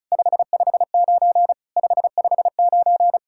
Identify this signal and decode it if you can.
Morse code, heard as a keyed tone, about 34 wpm, 690 Hz. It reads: 559 559
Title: SOTA QSO Practice 35wpm